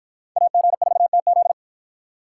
Morse code, heard as a keyed tone, about 50 wpm, 700 Hz. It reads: AG4TZ